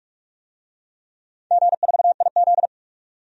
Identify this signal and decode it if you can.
Morse code, heard as a keyed tone, about 44 wpm, 700 Hz. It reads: G4IZ
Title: DXCC-100 Call Signs - 4 Characters - 45wpm